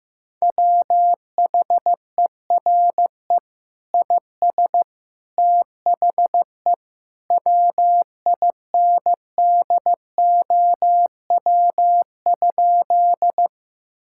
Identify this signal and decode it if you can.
Morse code, heard as a keyed tone, about 15 wpm, 700 Hz. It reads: WHERE IS THE WINDOW?
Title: Sentences from Top 1600 Words 15wpm